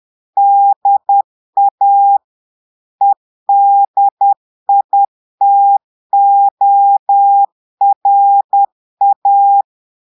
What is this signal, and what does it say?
Morse code, heard as a keyed tone, about 10 wpm, 800 Hz. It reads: DA EDITORA
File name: Portugais - texte en clair 1 - vitesse - 10 mots minute